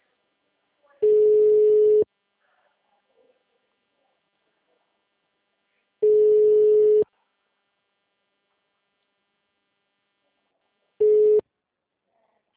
Ringback tone (Europe, 425 Hz).